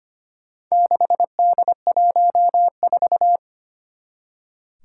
Morse code, heard as a keyed tone, about 25 wpm, 700 Hz. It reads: 6D14
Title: Four Challenging Letter-Number Combinations 25wpm